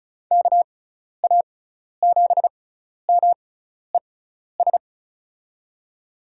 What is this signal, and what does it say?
Morse code, heard as a keyed tone, about 34 wpm, spaced as for 12 wpm, 690 Hz. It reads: KA7MES
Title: USA Call Signs - 6 Characters - 6x Character Spacing - 35wpm